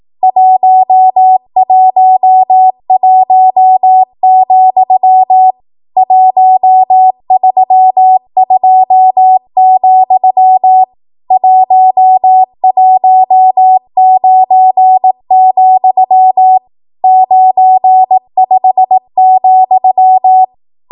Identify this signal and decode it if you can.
Morse code, heard as a keyed tone, about 18 wpm, 750 Hz. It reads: 111, 132, 119, 95,
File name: Archive / 18 WPM / 18WPM